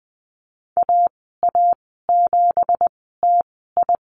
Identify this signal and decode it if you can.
Morse code, heard as a keyed tone, about 20 wpm, spaced as for 15 wpm, 700 Hz. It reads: AA7TI
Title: 193 Lesson 31g - Letter v - Call Signs 20wpm